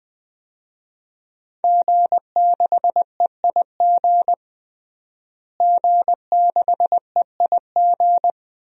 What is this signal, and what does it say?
Morse code, heard as a keyed tone, about 20 wpm, 700 Hz. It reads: G6EIG G6EIG
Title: DXCC-100 Call Signs - 5 Characters - Repeated 3x - 20wpm